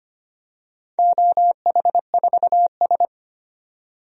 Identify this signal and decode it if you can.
Morse code, heard as a keyed tone, about 25 wpm, 700 Hz. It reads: OH4S
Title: DXCC-100 Call Signs - 4 Characters - 25wpm